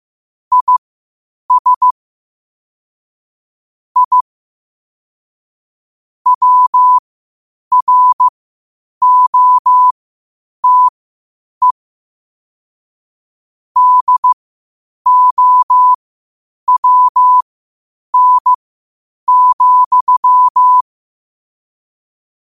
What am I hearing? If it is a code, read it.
Morse code, heard as a keyed tone, about 15 wpm, spaced as for 8 wpm, 1000 Hz. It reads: IS I WROTE DOWN,
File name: qotdfiles / 8wpm